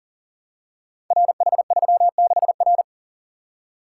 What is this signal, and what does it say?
Morse code, heard as a keyed tone, about 41 wpm, 690 Hz. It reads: RH36R
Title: Code Groups - 5 Letter:Numbers - 40wpm